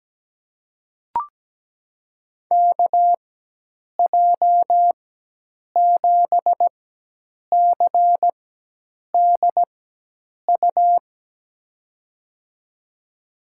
Morse code, heard as a keyed tone, about 17 wpm, spaced as for 8 wpm, 700 Hz. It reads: KJ7CDU
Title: USA Call Signs - 6 Characters - 4x Character Spacing - 17wpm